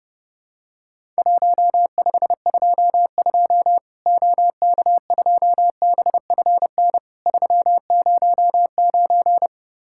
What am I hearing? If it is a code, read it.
Morse code, heard as a keyed tone, about 30 wpm, 700 Hz. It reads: 1522 OK2BFN 309
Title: Worked All Europe DX Contest - QTC Exchange - 30wpm